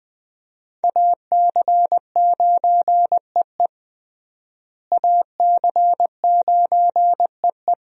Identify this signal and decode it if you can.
Morse code, heard as a keyed tone, about 21 wpm, 700 Hz. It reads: AC9EE AC9EE
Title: USA Call Signs - 5 Characters - Repeated 3x - 20wpm